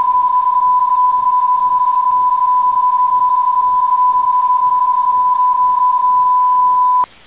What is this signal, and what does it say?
PSK31, about 31.25 baud.